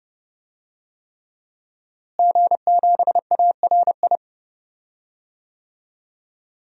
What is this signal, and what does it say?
Morse code, heard as a keyed tone, about 31 wpm, 700 Hz. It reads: G7ARI
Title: DXCC-100 Call Signs - 5 Characters - Repeated 3x - 30wpm